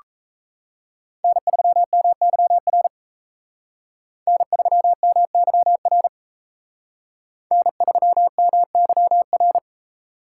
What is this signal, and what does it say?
Morse code, heard as a keyed tone, about 35 wpm, 700 Hz. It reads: N3MYR N3MYR N3MYR